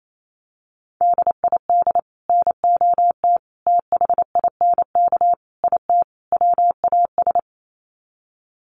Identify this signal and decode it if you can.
Morse code, heard as a keyed tone, about 28 wpm, 700 Hz. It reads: DID NOT THINK IT WAS
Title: Sets of 5 Words Encore 28wpm